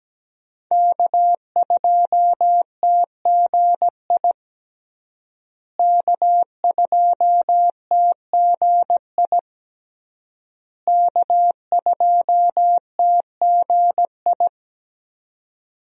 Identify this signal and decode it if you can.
Morse code, heard as a keyed tone, about 17 wpm, 700 Hz. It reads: K2TGI K2TGI K2TGI
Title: USA Call Signs - 5 Characters - Repeated 3x - 17wpm